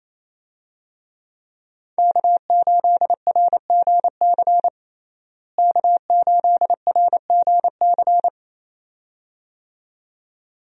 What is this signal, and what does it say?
Morse code, heard as a keyed tone, about 28 wpm, 700 Hz. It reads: K8RGC K8RGC
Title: USA Call Signs - 5 Characters - Repeated 2x - 28wpm